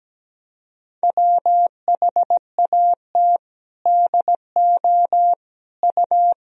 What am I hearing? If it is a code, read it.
Morse code, heard as a keyed tone, about 17 wpm, 700 Hz. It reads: WHAT DO U
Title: Sentences from Top 200 Words and Abbreviations 17wpm